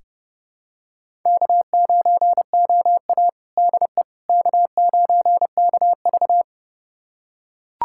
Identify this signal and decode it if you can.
Morse code, heard as a keyed tone, about 30 wpm, 700 Hz. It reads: K9OA DE K9KV
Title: USA Callsign DE Callsign - 4 Characters - 30wpm